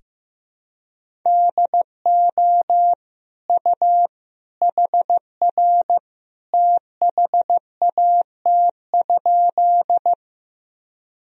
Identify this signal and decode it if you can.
Morse code, heard as a keyed tone, about 16 wpm, 700 Hz. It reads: DO U HR THAT?
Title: Sentences from Top 300 Words and Abbreviations 15wpm